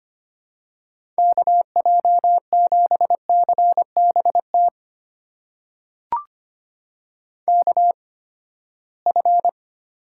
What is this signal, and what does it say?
Morse code, heard as a keyed tone, about 25 wpm, spaced as for 17 wpm, 700 Hz. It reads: KJ7CBT KF
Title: USA Call Signs - 6 Characters - 8x Character Spacing - 25wpm